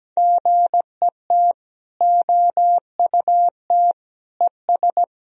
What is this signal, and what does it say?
Morse code, heard as a keyed tone, about 17 wpm, 700 Hz. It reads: GET OUT ES
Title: Sentences from Top 200 Words and Abbreviations 17wpm